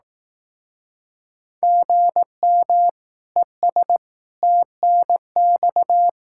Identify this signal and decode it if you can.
Morse code, heard as a keyed tone, about 18 wpm, 700 Hz. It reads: GM ES TNX
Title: Speed-Racing Common QSO Phrases Encore - 15wpm